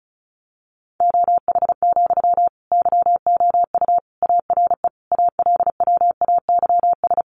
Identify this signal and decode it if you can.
Morse code, heard as a keyed tone, about 34 wpm, 700 Hz. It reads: OH, YOU ARE ALWAYS
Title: Sentences from Top 900 Words 35wpm